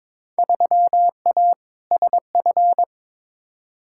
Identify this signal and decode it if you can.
Morse code, heard as a keyed tone, about 22 wpm, 700 Hz. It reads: 3A SF
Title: Field Day Exchanges Encore - 22wpm